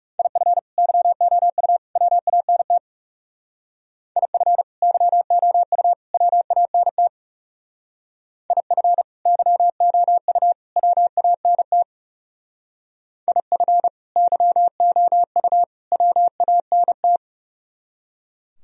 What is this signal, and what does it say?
Morse code, heard as a keyed tone, about 36 wpm, 700 Hz. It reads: IF YOU WANT IF YOU WANT IF YOU WANT IF YOU WANT
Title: Speed-Racing - Sets of 3 Words 30wpm